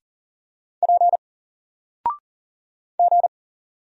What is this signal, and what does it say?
Morse code, heard as a keyed tone, about 40 wpm, 700 Hz. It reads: P G
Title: Single Letter-Number Alt 40wpm